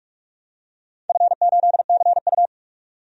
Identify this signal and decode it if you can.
Morse code, heard as a keyed tone, about 41 wpm, 700 Hz. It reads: F8CU
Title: DXCC-100 Call Signs - 4 Characters - 45wpm